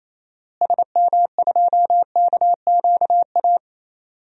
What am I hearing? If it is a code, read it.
Morse code, heard as a keyed tone, about 28 wpm, 700 Hz. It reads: SM2KQA
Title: DXCC-100 Call Signs - 6 Characters - Repeated 3x - 28wpm